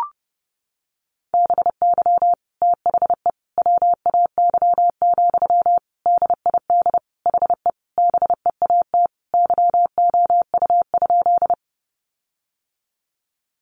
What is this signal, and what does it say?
Morse code, heard as a keyed tone, about 30 wpm, 700 Hz. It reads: BY THE WAY, DID HE BEAT YOU?
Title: Sentences from Top 1300 Words 30wpm